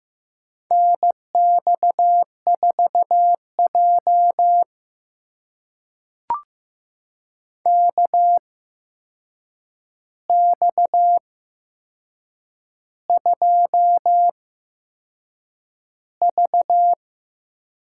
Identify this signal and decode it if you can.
Morse code, heard as a keyed tone, about 15 wpm, spaced as for 6 wpm, 700 Hz. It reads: NX4J KX2V
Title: USA Call Signs - 4 Characters - 8x Character Spacing - 15wpm